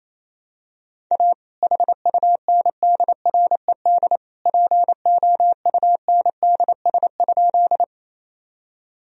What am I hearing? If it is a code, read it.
Morse code, heard as a keyed tone, about 28 wpm, 700 Hz. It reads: A HUNDRED POUNDS?
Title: Sentences from Top 1000 Words - Review Part 2 - 28wpm